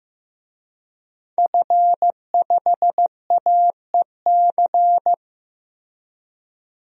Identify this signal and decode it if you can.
Morse code, heard as a keyed tone, about 16 wpm, 700 Hz. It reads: F5AEC
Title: DXCC-100 Call Signs - 5 Characters - Repeated 3x - 15wpm